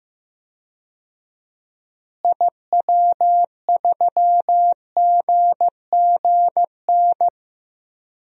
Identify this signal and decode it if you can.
Morse code, heard as a keyed tone, about 15 wpm, 700 Hz. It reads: IW3GGN
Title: DXCC-100 Call Signs - 6 Characters - Repeated 2x - 15wpm